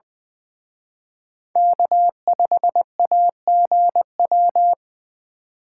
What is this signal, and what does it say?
Morse code, heard as a keyed tone, about 23 wpm, 700 Hz. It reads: K5AGW
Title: Speed Racing - USA Call Signs 20wpm